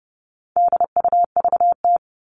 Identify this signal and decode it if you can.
Morse code, heard as a keyed tone, about 30 wpm, 700 Hz. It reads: DUVT
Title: All Cut Numbers - 4-Digit 30wpm